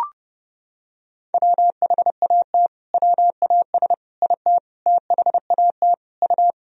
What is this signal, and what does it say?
Morse code, heard as a keyed tone, about 30 wpm, 700 Hz. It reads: WHAT WAS IT THAT U
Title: Sentences from Top 300 Words and Abbreviations 30wpm